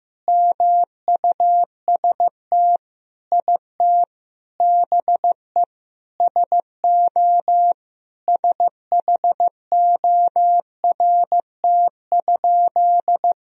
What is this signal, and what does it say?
Morse code, heard as a keyed tone, about 15 wpm, 700 Hz. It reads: MUST IT BE SO SHORT?
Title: Sentences from Top 1000 Words - Review Part 2 - 15wpm